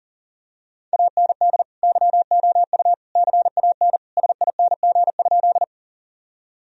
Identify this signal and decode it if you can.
Morse code, heard as a keyed tone, about 37 wpm, 700 Hz. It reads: AND YOU CAN SING?
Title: Sentences from Top 1000 Words - Review Part 2 - 40wpm